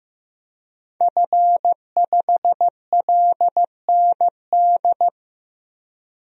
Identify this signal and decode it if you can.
Morse code, heard as a keyed tone, about 15 wpm, 700 Hz. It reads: F5LND
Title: DXCC-100 Call Signs - 5 Characters - 15wpm